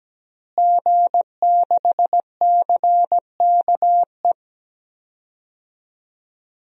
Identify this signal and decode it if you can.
Morse code, heard as a keyed tone, about 17 wpm, 700 Hz. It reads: G6CKE
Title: DXCC-100 Call Signs - 5 Characters - Repeated 2x - 17wpm